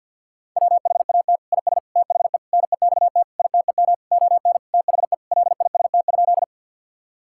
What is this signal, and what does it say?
Morse code, heard as a keyed tone, about 46 wpm, 700 Hz. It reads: WHAT IS THE NEXT ITEM ON THE LIST?
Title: Sentences from Top 1000 Words - Review Part 6 - 50wpm